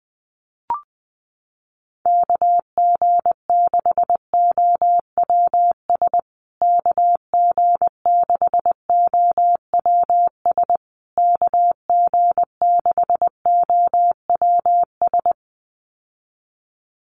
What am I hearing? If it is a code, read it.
Morse code, heard as a keyed tone, about 20 wpm, 700 Hz. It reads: KG6OWS KG6OWS KG6OWS
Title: USA Call Signs Repeated 3x 20wpm